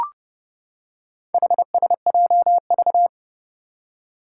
Morse code, heard as a keyed tone, about 30 wpm, 700 Hz. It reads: HSJV
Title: Four Challenging Letter-Number Combinations 30wpm